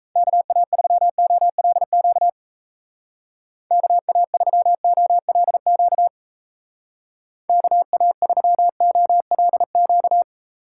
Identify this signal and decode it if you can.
Morse code, heard as a keyed tone, about 34 wpm, 700 Hz. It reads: KA3OLQ KA3OLQ KA3OLQ
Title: Speed Racing - USA 2x3 Call Signs 28wpm